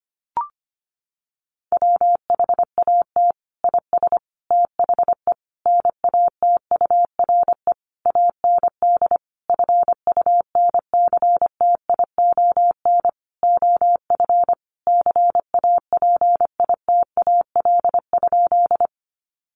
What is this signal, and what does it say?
Morse code, heard as a keyed tone, about 25 wpm, 700 Hz. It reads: WHAT IS THE NATURE AND FUNCTION OF CAPITAL?
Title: Sentences from Top 1100 Words 25wpm